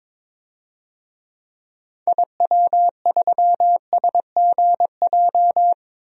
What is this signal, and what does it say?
Morse code, heard as a keyed tone, about 22 wpm, 700 Hz. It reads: IW3SGJ
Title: DXCC-100 Call Signs - 6 Characters - 22wpm